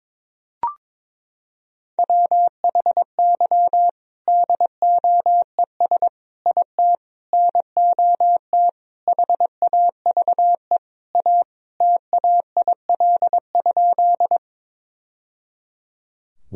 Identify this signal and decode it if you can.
Morse code, heard as a keyed tone, about 22 wpm, 700 Hz. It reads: WHY DOES IT NOT HAVE A TAIL?
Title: Sentences from Top 1000 Words - Review Part 3 - 22wpm